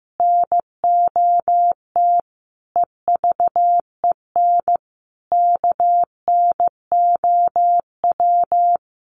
Morse code, heard as a keyed tone, about 15 wpm, 700 Hz. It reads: NOT EVEN KNOW
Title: Sets of 5 Words Encore2 15wpm